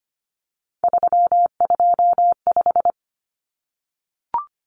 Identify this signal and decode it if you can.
Morse code, heard as a keyed tone, about 25 wpm, 700 Hz. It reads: 325
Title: Numbers - 3-Digit 25wpm